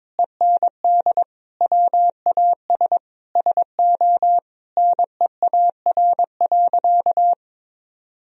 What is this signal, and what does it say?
Morse code, heard as a keyed tone, about 22 wpm, 700 Hz. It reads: END WAS SO NEAR.
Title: Sentences from Top 300 Words and Abbreviations 22wpm